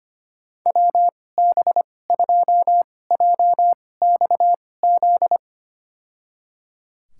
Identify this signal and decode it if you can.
Morse code, heard as a keyed tone, about 25 wpm, spaced as for 18 wpm, 700 Hz. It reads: WB2JXZ
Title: USA Call Signs - 6 Characters - 2x Character Spacing - 25wpm